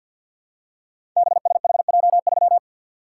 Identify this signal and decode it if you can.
Morse code, heard as a keyed tone, about 48 wpm, 700 Hz. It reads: BSHJ3
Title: Code Groups - 5 Letter:Numbers - 50wpm